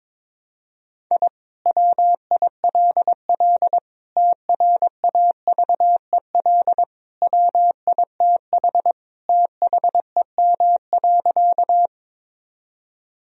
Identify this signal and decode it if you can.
Morse code, heard as a keyed tone, about 22 wpm, 700 Hz. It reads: I WILL TRAVEL WITH THEM.
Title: Sentences from Top 1000 Words - Review Part 2 - 22wpm